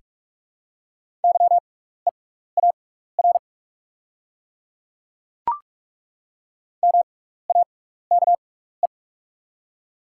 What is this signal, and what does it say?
Morse code, heard as a keyed tone, about 43 wpm, spaced as for 15 wpm, 700 Hz. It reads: YEAR MAKE
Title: 6x Character Spacing - Top 200 Words - 45wpm